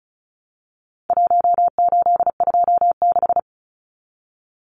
Morse code, heard as a keyed tone, about 34 wpm, 700 Hz. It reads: 1826
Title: Numbers - Contest Style - 4-Digit 35wpm